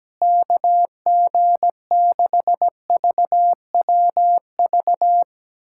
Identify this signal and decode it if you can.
Morse code, heard as a keyed tone, about 17 wpm, 700 Hz. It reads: KG6VWV
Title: USA Callsign DE Callsign - 6 Characters - 17wpm